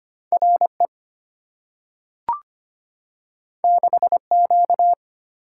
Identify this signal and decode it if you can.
Morse code, heard as a keyed tone, about 25 wpm, 700 Hz. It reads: RE 6Q
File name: Two Letter-Number Combinations - 25wpm